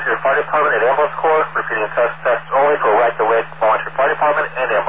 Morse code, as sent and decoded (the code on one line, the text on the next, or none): none